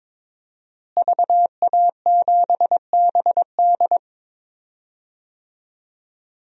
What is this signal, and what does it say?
Morse code, heard as a keyed tone, about 22 wpm, 700 Hz. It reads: VA7BD
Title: DXCC-100 Call Signs - 5 Characters - Repeated 2x - 22wpm